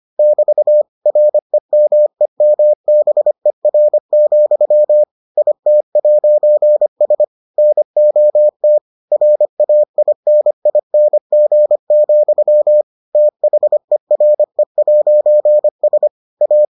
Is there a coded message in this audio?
Yes: Morse code from 0.2 to 16.6 s, heard as a keyed tone, about 25 wpm, 600 Hz. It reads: <BT> REMEMBER, IT'S NOT RAINING, THERE'S A